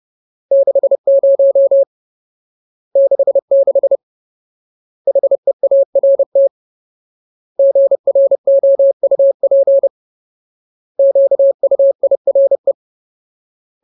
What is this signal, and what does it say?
Morse code, heard as a keyed tone, about 30 wpm, 550 Hz. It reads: 60 66 HEART GROUP QUIRE
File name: Ultimate Word List / 30WPM Ultimate